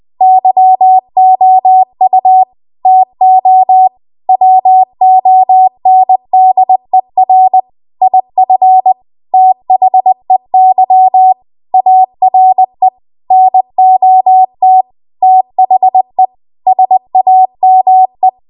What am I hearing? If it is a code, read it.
Morse code, heard as a keyed tone, about 20 wpm, 750 Hz. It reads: YOU TO WONDER IF THEY ARE NOT THE SAME
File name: Archive / 20 WPM / 20WPM